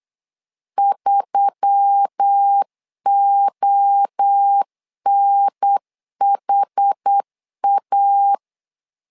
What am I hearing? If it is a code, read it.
Morse code, heard as a keyed tone, about 8 wpm, 800 Hz. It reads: 3ONHA
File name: LEZIONI / 4 Q1S3 8wpm 3037